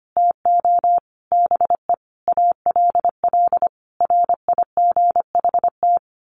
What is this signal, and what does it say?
Morse code, heard as a keyed tone, about 25 wpm, 700 Hz. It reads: TO BE ALL RIGHT
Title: Sets of 5 Words Encore2 25wpm